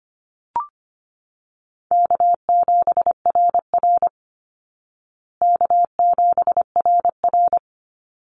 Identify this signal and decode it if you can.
Morse code, heard as a keyed tone, about 25 wpm, 700 Hz. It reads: K7RR K7RR
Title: Call Signs Repeated 2x - 25wpm